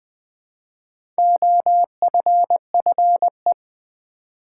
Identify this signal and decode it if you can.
Morse code, heard as a keyed tone, about 20 wpm, 700 Hz. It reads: OFFE
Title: Build A Word - 2 Letters At A Time - 8-Letter Words - 20wpm